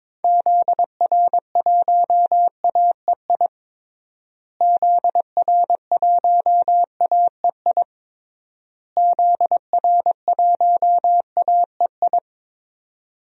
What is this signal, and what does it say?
Morse code, heard as a keyed tone, about 22 wpm, 700 Hz. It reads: ZR1AEI ZR1AEI ZR1AEI
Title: DXCC-100 Call Signs - 6 Characters - Repeated 3x - 22wpm